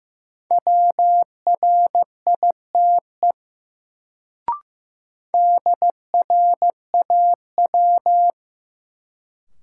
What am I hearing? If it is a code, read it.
Morse code, heard as a keyed tone, about 15 wpm, 700 Hz. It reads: WRITE DRAW
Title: Sentences from Top 300 Words - New Word Drill - 15wpm